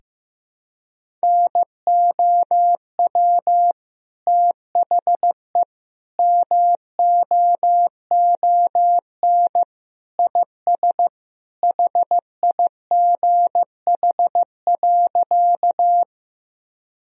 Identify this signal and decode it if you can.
Morse code, heard as a keyed tone, about 15 wpm, 700 Hz. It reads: NOW THE MOON IS HIGH.
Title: Sentences from Top 1000 Words - Review Part 3 - 15wpm